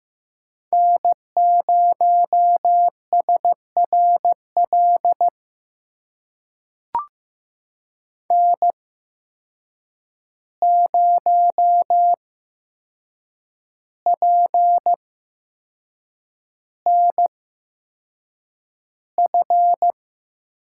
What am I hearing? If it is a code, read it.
Morse code, heard as a keyed tone, about 15 wpm, spaced as for 6 wpm, 700 Hz. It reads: N0SRL N0PNF
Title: USA Call Signs - 5 Characters - 8x Character Spacing - 15wpm